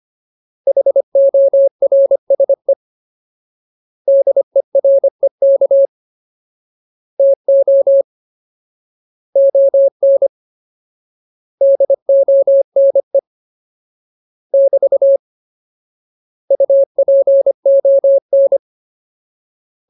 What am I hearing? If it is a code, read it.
Morse code, heard as a keyed tone, about 25 wpm, 550 Hz. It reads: HORSE DEREK TO ON DONE <BT> UPON
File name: Ultimate Word List / 25WPM Ultimate